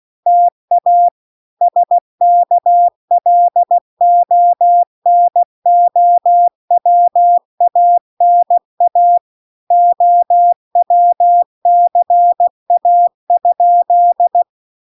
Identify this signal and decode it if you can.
Morse code, heard as a keyed tone, about 16 wpm, 700 Hz. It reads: TA SKLONOWANA OWCA?